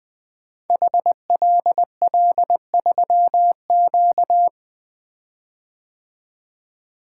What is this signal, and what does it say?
Morse code, heard as a keyed tone, about 20 wpm, 700 Hz. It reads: HLL3Q
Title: Code Groups - 5 Letter:Numbers - 20wpm